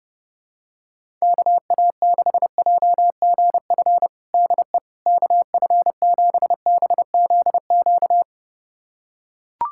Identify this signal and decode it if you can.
Morse code, heard as a keyed tone, about 30 wpm, 700 Hz. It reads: KA6JGF DE KF7BZQ
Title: USA Callsign DE Callsign - 6 Characters - 30wpm